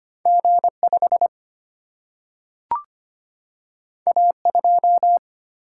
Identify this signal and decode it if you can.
Morse code, heard as a keyed tone, about 25 wpm, 700 Hz. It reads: G 5 A 2
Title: Single-Letter Single-Number 25wpm